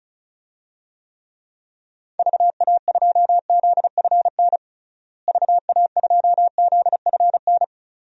Morse code, heard as a keyed tone, about 36 wpm, 700 Hz. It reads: VA2ZFN VA2ZFN
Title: DXCC-100 Call Signs - 6 Characters - Repeated 3x - 35wpm